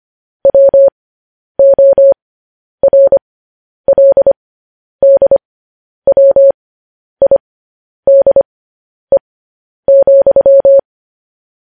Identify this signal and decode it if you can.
Morse code, heard as a keyed tone, about 25 wpm, spaced as for 10 wpm, 550 Hz. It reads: WORLDWIDE,